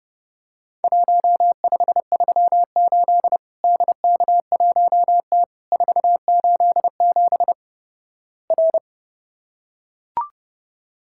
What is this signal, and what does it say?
Morse code, heard as a keyed tone, about 30 wpm, 700 Hz. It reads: 1538 DK1T 487 R
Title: Worked All Europe DX Contest - QTC Exchange - 30wpm